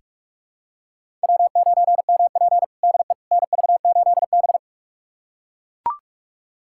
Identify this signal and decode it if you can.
Morse code, heard as a keyed tone, about 42 wpm, 700 Hz. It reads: W9MP DE NV8B
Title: USA Callsign DE Callsign - 4 Characters - 45wpm